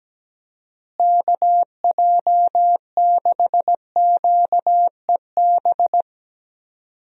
Morse code, heard as a keyed tone, about 17 wpm, 700 Hz. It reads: KJ6QEB
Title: USA Call Signs - 6 Characters - Repeated 2x - 17wpm